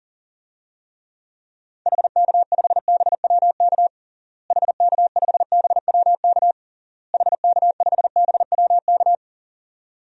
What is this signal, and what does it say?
Morse code, heard as a keyed tone, about 40 wpm, 700 Hz. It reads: HK5BWK HK5BWK HK5BWK
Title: DXCC-100 Call Signs - 6 Characters - Repeated 3x - 40wpm